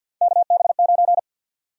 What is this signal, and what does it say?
Morse code, heard as a keyed tone, about 52 wpm, 700 Hz. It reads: KB9
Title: USA Call Sign Prefixes - Repeated 2x - 50wpm